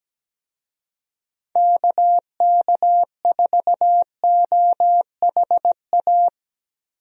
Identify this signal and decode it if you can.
Morse code, heard as a keyed tone, about 17 wpm, 700 Hz. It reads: KK4OHA
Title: USA Call Signs - 6 Characters - Repeated 3x - 17wpm